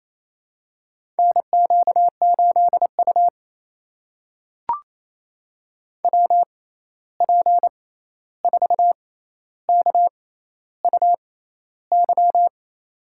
Morse code, heard as a keyed tone, about 28 wpm, spaced as for 13 wpm, 700 Hz. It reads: NQ8U WP4KUY
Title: USA Call Signs - 6x Character Spacing - 28wpm